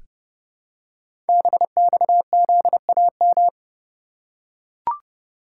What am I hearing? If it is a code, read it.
Morse code, heard as a keyed tone, about 29 wpm, 700 Hz. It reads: BXZAM
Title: Code Groups - 5 Letters - 30wpm